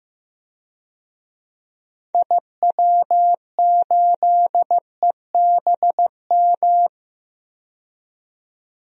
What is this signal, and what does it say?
Morse code, heard as a keyed tone, about 15 wpm, 700 Hz. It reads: IW8EBM
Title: DXCC-100 Call Signs - 6 Characters - 15wpm